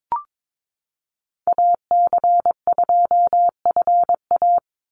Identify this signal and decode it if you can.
Morse code, heard as a keyed tone, about 22 wpm, 700 Hz. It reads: AC2FA
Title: Call Signs - 22wpm